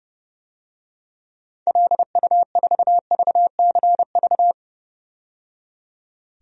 Morse code, heard as a keyed tone, about 30 wpm, 700 Hz. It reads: LU4VCV
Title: DXCC-100 Call Signs - 6 Characters - 30wpm